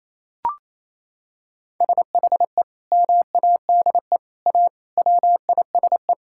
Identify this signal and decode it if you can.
Morse code, heard as a keyed tone, about 28 wpm, 700 Hz. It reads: SHE MADE A WISE
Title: Sentences from Top 1600 Words 28wpm